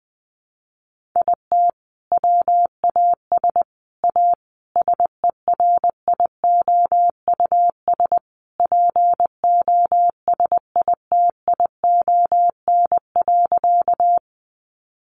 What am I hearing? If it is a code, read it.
Morse code, heard as a keyed tone, about 20 wpm, 700 Hz. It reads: IT WAS A SERIOUS POSITION.
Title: Sentences from Top 1100 Words 20wpm